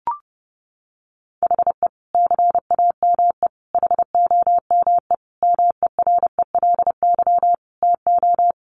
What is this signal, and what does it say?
Morse code, heard as a keyed tone, about 28 wpm, 700 Hz. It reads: HE CAME HOME MERELY TO
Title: Sentences from Top 1400 Words 30wpm